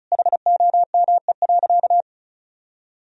Morse code, heard as a keyed tone, about 36 wpm, 700 Hz. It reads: HOME.
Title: Sentences from Top 1000 Words - Review Part 1 - 35wpm